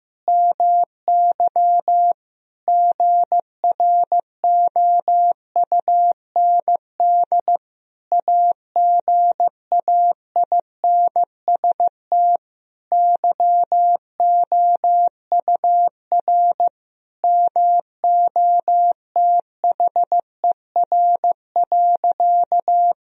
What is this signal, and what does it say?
Morse code, heard as a keyed tone, about 15 wpm, 700 Hz. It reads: MY GROUND AGAINST YOUR MOTHER.
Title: Sentences from Top 1000 Words - Review Part 2 - 15wpm